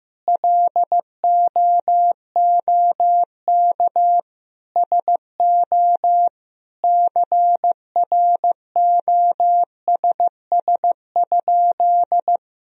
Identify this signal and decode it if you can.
Morse code, heard as a keyed tone, about 15 wpm, 700 Hz. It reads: LOOK SO CROSS?
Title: Sentences from Top 1000 Words - Review Part 1 - 15wpm